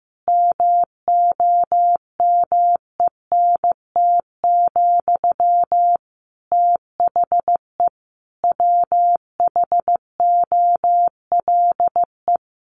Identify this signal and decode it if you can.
Morse code, heard as a keyed tone, about 15 wpm, 700 Hz. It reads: MOMENT, THE WHOLE
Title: Sentences from Top 1000 Words 15wpm